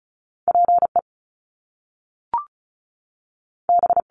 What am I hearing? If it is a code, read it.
Morse code, heard as a keyed tone, about 36 wpm, 700 Hz. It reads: P E B
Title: Top Two Letter Combinations in Words 35wpm